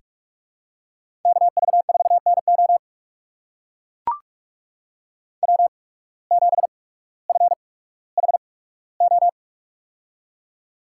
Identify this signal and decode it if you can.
Morse code, heard as a keyed tone, about 44 wpm, spaced as for 19 wpm, 690 Hz. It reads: KV4NO W7FHO
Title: USA Call Signs - 5 Characters - 8x Character Spacing - 45wpm